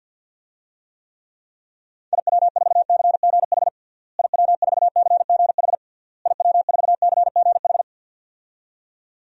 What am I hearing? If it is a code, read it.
Morse code, heard as a keyed tone, about 49 wpm, 700 Hz. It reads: IW4CGH IW4CGH IW4CGH
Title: DXCC-100 Call Signs - 6 Characters - Repeated 3x - 50wpm